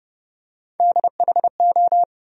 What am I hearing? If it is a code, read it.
Morse code, heard as a keyed tone, about 30 wpm, 700 Hz. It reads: DHO
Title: USA Call Sign Postfixes - 3 Letters 30wpm